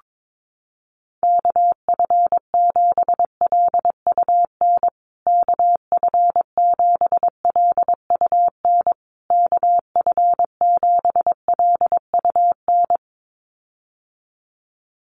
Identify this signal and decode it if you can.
Morse code, heard as a keyed tone, about 22 wpm, 690 Hz. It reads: KF7LUN KF7LUN KF7LUN
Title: USA Call Signs Repeated 3x 22wpm